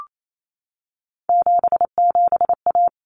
Morse code, heard as a keyed tone, about 28 wpm, 700 Hz. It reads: 77A
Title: Common Cut Numbers - 3-Digit 28wpm